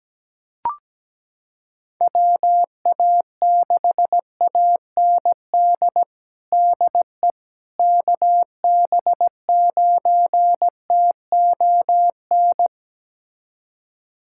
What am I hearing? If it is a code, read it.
Morse code, heard as a keyed tone, about 17 wpm, 700 Hz. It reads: WA6AND DE KB9TON
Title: USA Callsign DE Callsign 17wpm